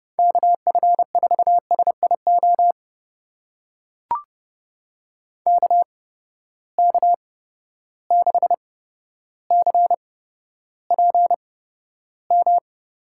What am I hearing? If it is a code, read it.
Morse code, heard as a keyed tone, about 30 wpm, spaced as for 13 wpm, 700 Hz. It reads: KF4SIO KK6CPM
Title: USA Call Signs - 6 Characters - 8x Character Spacing - 30wpm